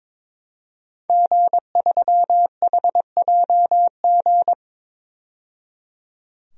Morse code, heard as a keyed tone, about 22 wpm, 700 Hz. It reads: G3HJG
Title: DXCC-100 Call Signs - 5 Characters - 22wpm